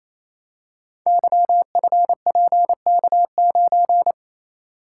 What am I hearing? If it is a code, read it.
Morse code, heard as a keyed tone, about 29 wpm, 700 Hz. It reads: YFPK9
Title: Code Groups - 5 Letter:Numbers - 28wpm